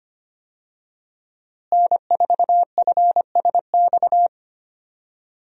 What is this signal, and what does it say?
Morse code, heard as a keyed tone, about 25 wpm, 700 Hz. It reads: N4FSX
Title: USA Call Signs - 5 Characters - Repeated 3x - 25wpm